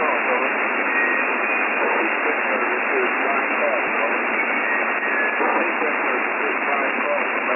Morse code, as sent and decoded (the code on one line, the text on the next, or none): none